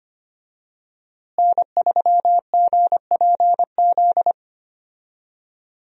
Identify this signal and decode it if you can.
Morse code, heard as a keyed tone, about 25 wpm, 700 Hz. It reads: N3GPZ
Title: USA Call Signs - 5 Characters - Repeated 2x - 25wpm